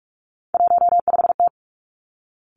Morse code, heard as a keyed tone, about 46 wpm, 700 Hz. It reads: 15T
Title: Three Challenging Letter-Number Combinations 45wpm